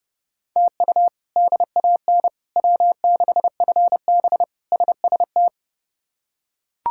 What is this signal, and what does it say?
Morse code, heard as a keyed tone, about 30 wpm, 700 Hz. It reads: TU DAN W6FB SST
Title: K1USN SST QSO Practice - 30wpm